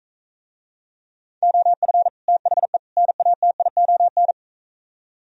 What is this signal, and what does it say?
Morse code, heard as a keyed tone, about 45 wpm, 700 Hz. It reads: OF THE NATION